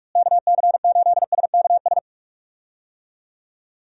Morse code, heard as a keyed tone, about 46 wpm, 700 Hz. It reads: KC8SKS
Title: USA Call Signs - 6 Characters - Repeated 3x - 45wpm